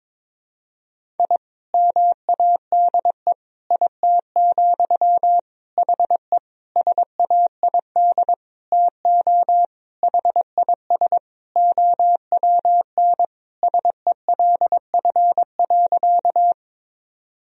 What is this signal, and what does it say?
Morse code, heard as a keyed tone, about 21 wpm, 700 Hz. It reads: I MADE IT, HE SAID TO HIS OWN SELF.
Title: Sentences from Top 1000 Words - Review Part 1 - 22wpm